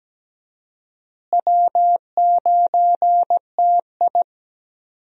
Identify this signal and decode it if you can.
Morse code, heard as a keyed tone, about 17 wpm, 700 Hz. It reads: W9TI
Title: USA Call Signs - 4 Characters - Repeated 3x - 17wpm